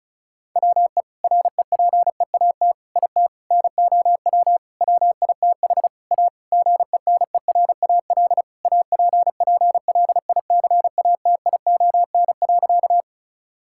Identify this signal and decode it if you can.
Morse code, heard as a keyed tone, about 34 wpm, 700 Hz. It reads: WE REPEAT IT NOW WITH A GENERAL APPLICATION.
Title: Sentences from Top 1600 Words 35wpm